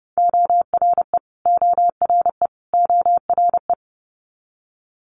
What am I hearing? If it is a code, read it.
Morse code, heard as a keyed tone, about 31 wpm, 700 Hz. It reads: ORE ORE ORE
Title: Multi-letter Phonemes Repeated 3x 30wpm